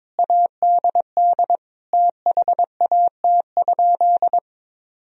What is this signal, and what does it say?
Morse code, heard as a keyed tone, about 22 wpm, 700 Hz. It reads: ADD THAT?
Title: Sentences from Top 1000 Words - Review Part 1 - 22wpm